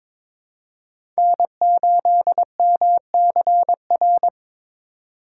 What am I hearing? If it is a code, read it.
Morse code, heard as a keyed tone, about 22 wpm, 700 Hz. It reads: N8MCR
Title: USA Call Signs - 5 Characters - 3x Character Spacing - 22wpm